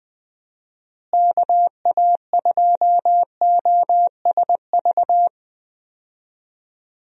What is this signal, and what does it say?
Morse code, heard as a keyed tone, about 20 wpm, 700 Hz. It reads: KA2OSV
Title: USA Call Signs - 6 Characters - Repeated 2x - 20wpm